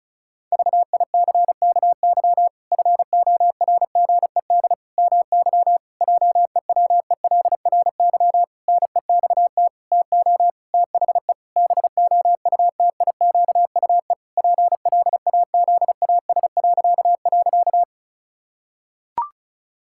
Morse code, heard as a keyed tone, about 36 wpm, 700 Hz. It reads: VICKY FORGED MY JEWELRY NEXT TO THE BOUTIQUE PLAZAS..
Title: Pangram Challenge v2 - 35wpm